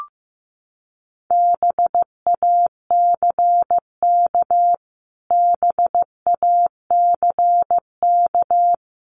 Morse code, heard as a keyed tone, about 15 wpm, 700 Hz. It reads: BACK BACK
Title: Top 500 Words Repeated 15wpm